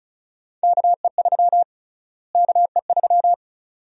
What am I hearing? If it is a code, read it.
Morse code, heard as a keyed tone, about 34 wpm, 700 Hz. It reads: KE3 KE3
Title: USA Call Sign Prefixes - Repeated 2x - 35wpm